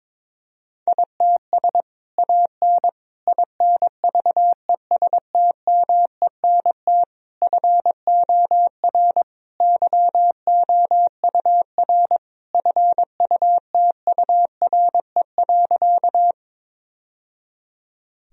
Morse code, heard as a keyed tone, about 22 wpm, 700 Hz. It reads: ITS AN INVESTMENT FOR YOUR FUTURE.
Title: Sentences from Top 1000 Words - Review Part 4 - 22wpm